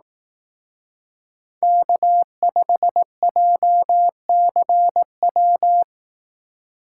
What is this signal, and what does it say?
Morse code, heard as a keyed tone, about 18 wpm, 700 Hz. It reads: K5JCW
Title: Speed Racing - USA 1x3 Call Signs 15wpm